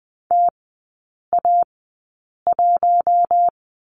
Morse code, heard as a keyed tone, about 20 wpm, spaced as for 8 wpm, 700 Hz. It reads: TA1
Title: 053 Lesson 10e - Letter r - Three Characters 20wpm